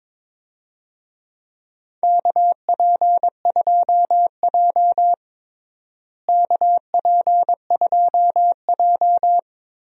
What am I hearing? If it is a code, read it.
Morse code, heard as a keyed tone, about 22 wpm, 700 Hz. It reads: KP2J KP2J
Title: DXCC-100 Call Signs - 4 Characters - Repeated 2x - 22wpm